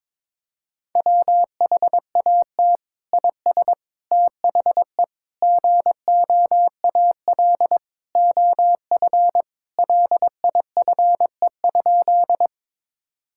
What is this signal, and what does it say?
Morse code, heard as a keyed tone, about 22 wpm, 700 Hz. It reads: WHAT IS THE GOAL OF LIFE?
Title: Sentences from Top 1000 Words - Review Part 5 - 22wpm